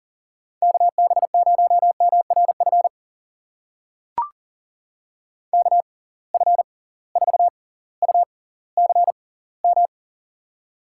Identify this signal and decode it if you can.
Morse code, heard as a keyed tone, about 39 wpm, spaced as for 20 wpm, 700 Hz. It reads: KB0MRF KF4UCM
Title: USA Call Signs - 6 Characters - 6x Character Spacing - 40wpm